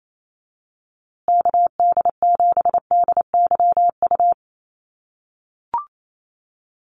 Morse code, heard as a keyed tone, about 28 wpm, 700 Hz. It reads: KD7DYU
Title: USA Call Signs Repeated 3x 28wpm